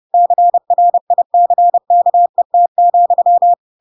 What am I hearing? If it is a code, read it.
Morse code, heard as a keyed tone, about 30 wpm, 700 Hz. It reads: CRICKET,
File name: Pinocchio / Pinocchio ch4 30 30 ik1hge